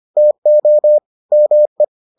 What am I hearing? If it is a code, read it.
Morse code, heard as a keyed tone, about 25 wpm, 600 Hz. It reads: TO ME